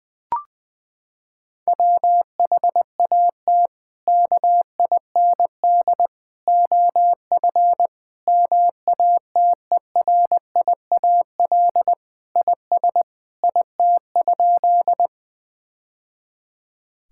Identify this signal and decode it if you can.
Morse code, heard as a keyed tone, about 21 wpm, 700 Hz. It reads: WHAT KIND OF MATERIAL IS IT?
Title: Sentences from Top 1000 Words - Review Part 5 - 20wpm